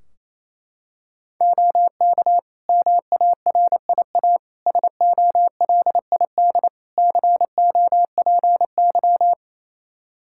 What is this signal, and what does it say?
Morse code, heard as a keyed tone, about 27 wpm, 700 Hz. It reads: OK MARIA SOLID COPY
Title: Practice QSO - Exchange 3 - More Info - 28wpm